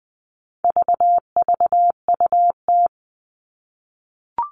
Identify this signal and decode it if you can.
Morse code, heard as a keyed tone, about 20 wpm, 700 Hz. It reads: VVUT
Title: All Cut Numbers - 4-Digit 20wpm